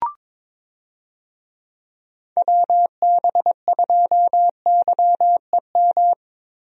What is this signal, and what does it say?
Morse code, heard as a keyed tone, about 22 wpm, 700 Hz. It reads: WB2YEM
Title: USA Call Signs - 6 Characters - Repeated 3x - 22wpm